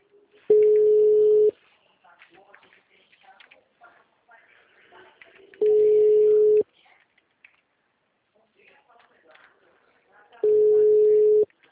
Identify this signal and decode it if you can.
Ringback tone (Europe, 425 Hz), with speech.
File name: out- -2006- - - .46454